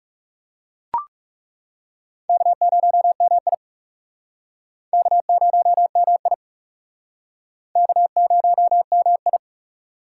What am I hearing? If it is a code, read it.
Morse code, heard as a keyed tone, about 39 wpm, 700 Hz. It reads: K0MI K0MI K0MI